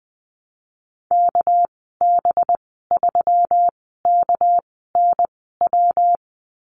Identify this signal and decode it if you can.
Morse code, heard as a keyed tone, about 19 wpm, 700 Hz. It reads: K B 3 K N W
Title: 207 Lesson 33g - Letter j - Call Signs 20wpm